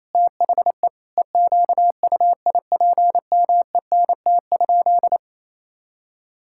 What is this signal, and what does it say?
Morse code, heard as a keyed tone, about 28 wpm, 700 Hz. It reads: THE EQUIPMENT?
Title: Sentences from Top 1000 Words - Review Part 5 - 28wpm